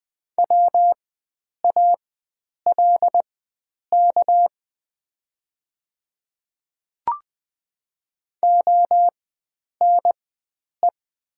Morse code, heard as a keyed tone, about 20 wpm, spaced as for 9 wpm, 700 Hz. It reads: WALK ONE
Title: 4x Character Spacing - Top 300 Words - 20wpm